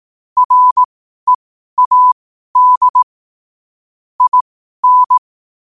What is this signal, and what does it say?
Morse code, heard as a keyed tone, about 18 wpm, spaced as for 12 wpm, 1000 Hz. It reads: READ IN
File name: qotdfiles / 12wpm